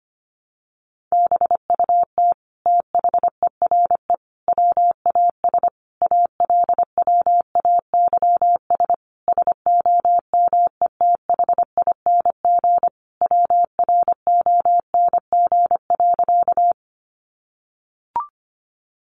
Morse code, heard as a keyed tone, about 25 wpm, 700 Hz. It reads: BUT THERE WAS ALWAYS SOMETHING WRONG.
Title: Sentences from Top 900 Words 25wpm